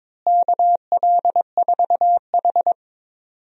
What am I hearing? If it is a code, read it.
Morse code, heard as a keyed tone, about 22 wpm, 700 Hz. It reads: KL4H
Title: USA Call Signs - 4 Characters - Repeated 3x - 22wpm